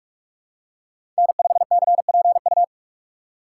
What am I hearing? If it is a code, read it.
Morse code, heard as a keyed tone, about 46 wpm, 700 Hz. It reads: N5CPU
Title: USA Call Signs - 5 Characters - 4x Character Spacing - 45wpm